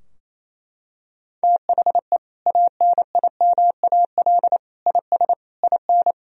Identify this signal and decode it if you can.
Morse code, heard as a keyed tone, about 28 wpm, 700 Hz. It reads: THE ANIMAL IS IN
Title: Sentences from Top 1000 Words - Review Part 3 - 28wpm